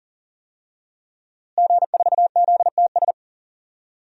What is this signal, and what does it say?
Morse code, heard as a keyed tone, about 38 wpm, 700 Hz. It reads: G4ZTS
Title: DXCC-100 Call Signs - 5 Characters - Repeated 3x - 40wpm